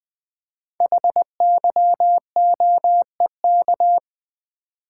Morse code, heard as a keyed tone, about 20 wpm, 700 Hz. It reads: HYOEK
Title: Code Groups - 5 Letters - 20wpm